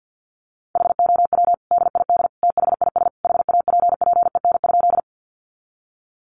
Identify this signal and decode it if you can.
Morse code, heard as a keyed tone, about 50 wpm, 700 Hz. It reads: HOW DID THIS HAPPEN?
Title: Sentences from Top 400 Words 50wpm